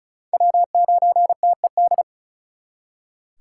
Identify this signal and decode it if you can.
Morse code, heard as a keyed tone, about 44 wpm, 700 Hz. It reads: W9TED
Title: Speed Racing - USA 1x3 Call Signs 35wpm